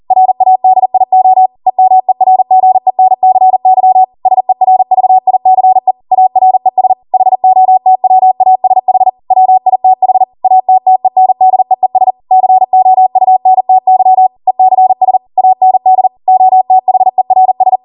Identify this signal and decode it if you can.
Morse code, heard as a keyed tone, about 40 wpm, 750 Hz. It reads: RADIO EMERGENCY SERVICE ARES HOTWASH WITH ATTENDEES COUNTY ECS AND OTHERS